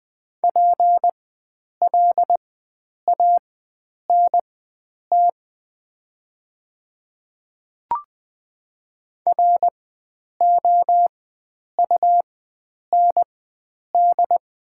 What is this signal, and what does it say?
Morse code, heard as a keyed tone, about 21 wpm, spaced as for 9 wpm, 700 Hz. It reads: PLANT ROUND
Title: 4x Character Spacing - Top 300 Words - 20wpm